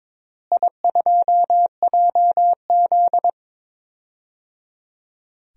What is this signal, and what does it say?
Morse code, heard as a keyed tone, about 22 wpm, 700 Hz. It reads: I2JZ
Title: DXCC-100 Call Signs - 4 Characters - Repeated 3x - 22wpm